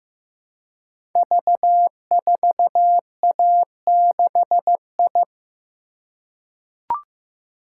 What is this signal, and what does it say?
Morse code, heard as a keyed tone, about 15 wpm, 700 Hz. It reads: V4A6I
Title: Code Groups - 5 Letter:Numbers - 15wpm